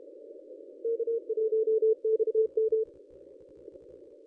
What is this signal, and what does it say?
Morse code, heard as a keyed tone, about 32 wpm, 440 Hz. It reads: K1XM
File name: contests / ss-cw-k5tr-op-n5rz -1100